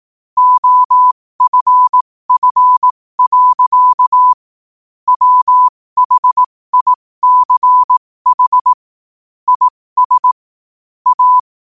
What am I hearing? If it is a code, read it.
Morse code, heard as a keyed tone, about 18 wpm, spaced as for 15 wpm, 1000 Hz. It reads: OFF. WHICH IS A
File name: qotdfiles / 15wpm